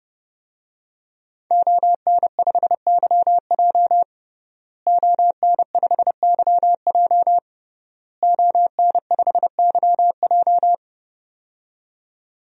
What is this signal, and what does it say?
Morse code, heard as a keyed tone, about 32 wpm, 700 Hz. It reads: ON5YJ ON5YJ ON5YJ
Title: DXCC-100 Call Signs - 5 Characters - Repeated 3x - 30wpm